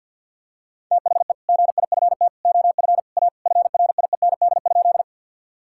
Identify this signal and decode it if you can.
Morse code, heard as a keyed tone, about 50 wpm, 700 Hz. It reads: THE GIFT OF A FRIEND?
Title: Sentences from Top 1000 Words - Review Part 6 - 50wpm